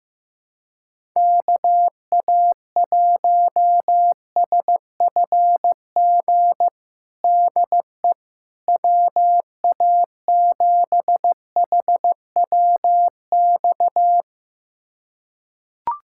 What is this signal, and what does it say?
Morse code, heard as a keyed tone, about 16 wpm, 700 Hz. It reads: KA1SFG DE WA7HWX
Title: USA Callsign DE Callsign 15wpm